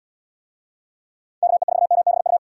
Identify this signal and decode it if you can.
Morse code, heard as a keyed tone, about 78 wpm, 700 Hz. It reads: KE4MBU